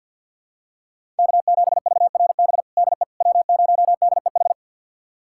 Morse code, heard as a keyed tone, about 48 wpm, 700 Hz. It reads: K7VRB DE W0DEH
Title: USA Callsign DE Callsign - 5 Characters - 50wpm